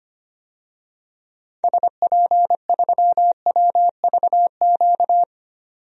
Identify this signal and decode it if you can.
Morse code, heard as a keyed tone, about 25 wpm, 700 Hz. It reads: SP3WVQ
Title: DXCC-100 Call Signs - 6 Characters - 25wpm